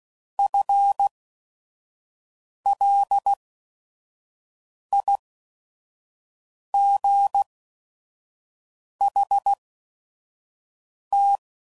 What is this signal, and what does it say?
Morse code, heard as a keyed tone, about 16 wpm, spaced as for 5 wpm, 800 Hz. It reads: FLIGHT